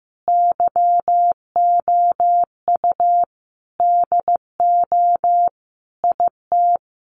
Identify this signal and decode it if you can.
Morse code, heard as a keyed tone, about 15 wpm, 700 Hz. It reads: YOU DO IT
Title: Sentences from Top 1200 Words 15wpm